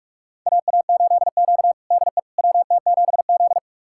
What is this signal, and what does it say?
Morse code, heard as a keyed tone, about 44 wpm, 690 Hz. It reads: AA8Q DE WT7Z
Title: USA Callsign DE Callsign - 4 Characters - 45wpm